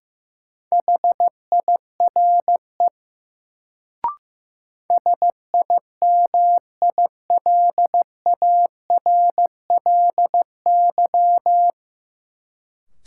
Morse code, heard as a keyed tone, about 15 wpm, 700 Hz. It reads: HIRE SIMILARLY
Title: Sentences from Top 1400 Words - New Word Drill Alt - 15wpm